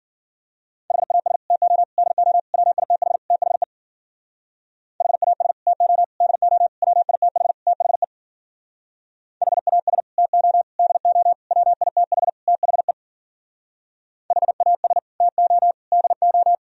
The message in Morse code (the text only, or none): HAS TO DO WITH THE HAS TO DO WITH THE HAS TO DO WITH THE HAS TO DO